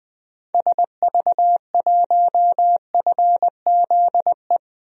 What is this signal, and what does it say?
Morse code, heard as a keyed tone, about 20 wpm, 700 Hz. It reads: SV1FZE
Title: DXCC-100 Call Signs - 6 Characters - Repeated 3x - 20wpm